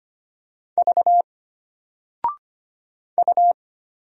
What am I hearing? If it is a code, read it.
Morse code, heard as a keyed tone, about 25 wpm, 700 Hz. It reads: V U
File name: Single-Letter-Number-25wpm